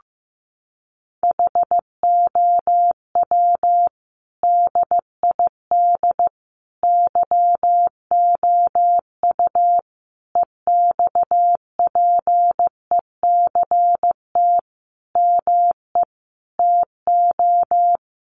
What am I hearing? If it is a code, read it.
Morse code, heard as a keyed tone, about 15 wpm, 700 Hz. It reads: HOW DID YOU EXPECT ME TO
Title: Sentences from Top 1100 Words 15wpm